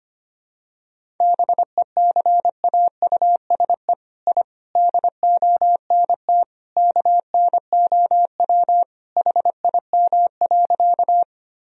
Morse code, heard as a keyed tone, about 25 wpm, 700 Hz. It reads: BECAUSE I DONT KNOW HIM.
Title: Sentences from Top 1000 Words - Review Part 3 - 25wpm